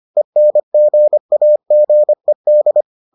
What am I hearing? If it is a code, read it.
Morse code, heard as a keyed tone, about 25 wpm, 600 Hz. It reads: ENGAGED